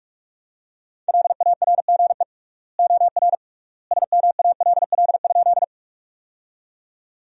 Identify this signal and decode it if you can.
Morse code, heard as a keyed tone, about 46 wpm, 700 Hz. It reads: LARGE OR SMALL?
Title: 3x Word-Spacing - Sentences from Top 200 Words - 45wpm